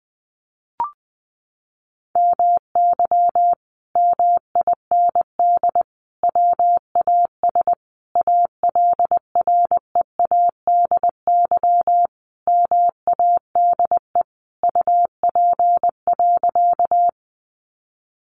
Morse code, heard as a keyed tone, about 20 wpm, 700 Hz. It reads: MY MIND WAS ALREADY MADE UP.
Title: Sentences from Top 700 Words 20wpm